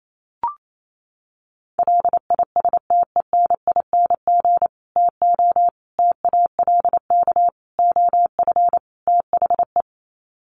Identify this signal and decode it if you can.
Morse code, heard as a keyed tone, about 28 wpm, 700 Hz. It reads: LISTENING TO TALK OF THE
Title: Sets of 5 Words 28wpm